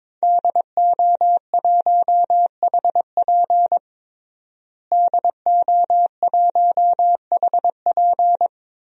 Morse code, heard as a keyed tone, about 22 wpm, 700 Hz. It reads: DO1HP DO1HP
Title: DXCC-100 Call Signs - 5 Characters - Repeated 2x - 22wpm